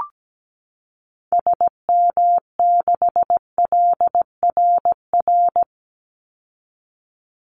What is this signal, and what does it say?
Morse code, heard as a keyed tone, about 17 wpm, 700 Hz. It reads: SM6LRR
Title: Call Signs Repeated 3x - 17wpm